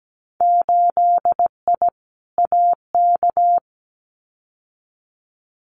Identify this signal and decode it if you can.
Morse code, heard as a keyed tone, about 17 wpm, 700 Hz. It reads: 8I AK
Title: Winter Field Day Exchanges Encore 17wpm